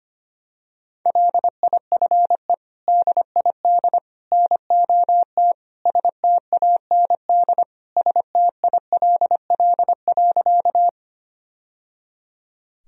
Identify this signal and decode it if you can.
Morse code, heard as a keyed tone, about 25 wpm, 700 Hz. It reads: LIFE DID NOT STAND STILL.
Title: Sentences from Top 1000 Words - Review Part 1 - 25wpm